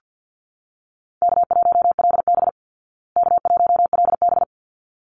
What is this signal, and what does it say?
Morse code, heard as a keyed tone, about 50 wpm, 700 Hz. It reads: K1LB K1LB
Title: Easy Call Signs Repeated 2x - 50wpm